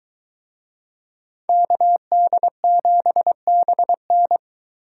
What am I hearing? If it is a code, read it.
Morse code, heard as a keyed tone, about 22 wpm, 700 Hz. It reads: KD7BN